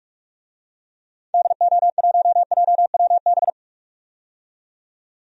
Morse code, heard as a keyed tone, about 41 wpm, 700 Hz. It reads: DO1JWB
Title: DXCC-100 Call Signs - 6 Characters - 45wpm